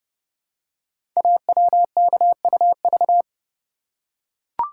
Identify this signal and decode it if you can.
Morse code, heard as a keyed tone, about 28 wpm, 700 Hz. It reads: AWKUV
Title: Code Groups - 5 Letters - 30wpm